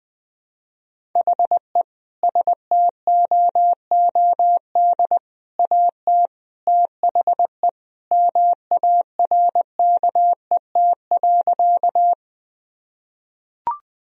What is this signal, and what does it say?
Morse code, heard as a keyed tone, about 20 wpm, 700 Hz. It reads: HE STOOD AT THE MARKET.
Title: Sentences from Top 1000 Words - Review Part 3 - 20wpm